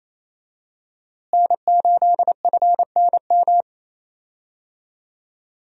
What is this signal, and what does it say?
Morse code, heard as a keyed tone, about 28 wpm, 700 Hz. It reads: N8FNM
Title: USA Call Signs - 5 Characters - Repeated 3x - 28wpm